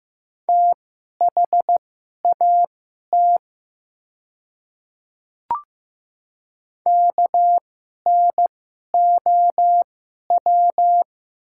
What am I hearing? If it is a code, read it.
Morse code, heard as a keyed tone, about 15 wpm, spaced as for 11 wpm, 700 Hz. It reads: THAT KNOW
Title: 2x Character Spacing - Top 100 Words - 15wpm